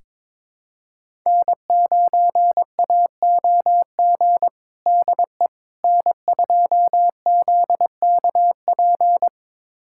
Morse code, heard as a keyed tone, about 21 wpm, 700 Hz. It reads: N9AOG DE N2ZKP
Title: USA Callsign DE Callsign - 5 Characters - 22wpm